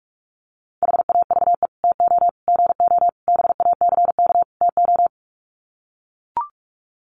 Morse code, heard as a keyed tone, about 44 wpm, 700 Hz. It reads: HAVE TO GO BACK TO
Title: Sets of 5 Words Encore2 45wpm